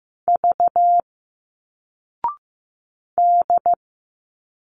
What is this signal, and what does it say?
Morse code, heard as a keyed tone, about 15 wpm, 700 Hz. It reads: V D
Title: Single Letters 15wpm